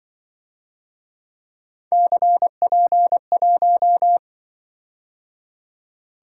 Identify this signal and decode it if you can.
Morse code, heard as a keyed tone, about 23 wpm, 700 Hz. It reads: CP1
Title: Speed-Racing - Three Letter-Number Combinations - 20wpm